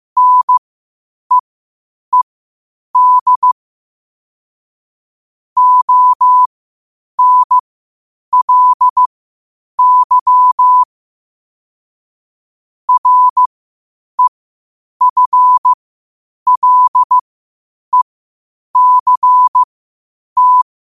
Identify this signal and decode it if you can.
Morse code, heard as a keyed tone, about 15 wpm, spaced as for 8 wpm, 1000 Hz. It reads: NEED ONLY REFLECT
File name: qotdfiles / 8wpm